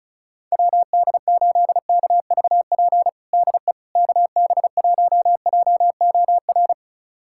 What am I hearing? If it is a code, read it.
Morse code, heard as a keyed tone, about 34 wpm, 700 Hz. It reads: WD8KVP DE KB1JOR
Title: USA Callsign DE Callsign - 6 Characters - 35wpm